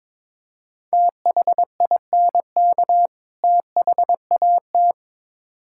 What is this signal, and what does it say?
Morse code, heard as a keyed tone, about 22 wpm, 700 Hz. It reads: THINK THAT
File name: Build A Phrase - 2-Word Phrases - 22wpm